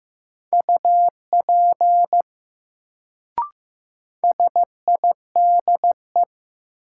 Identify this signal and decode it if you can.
Morse code, heard as a keyed tone, about 15 wpm, 700 Hz. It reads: UP SIDE
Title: Sets of 2 Words - New Word Drill - 15wpm